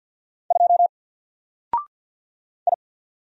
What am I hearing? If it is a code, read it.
Morse code, heard as a keyed tone, about 50 wpm, 700 Hz. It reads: 2 I
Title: Single Letter-Number 50wpm